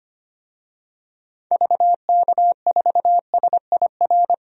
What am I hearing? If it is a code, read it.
Morse code, heard as a keyed tone, about 25 wpm, 700 Hz. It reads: VK4SIR
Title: DXCC-100 Call Signs - 6 Characters - 25wpm